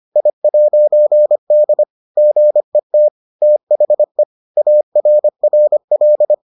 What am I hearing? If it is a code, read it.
Morse code, heard as a keyed tone, about 25 wpm, 600 Hz. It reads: I'D GET THE ARRL